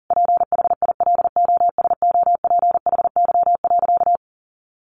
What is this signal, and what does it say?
Morse code, heard as a keyed tone, about 40 wpm, 700 Hz. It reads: PHILOSOPHY.
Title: Sentences from Top 900 Words 40wpm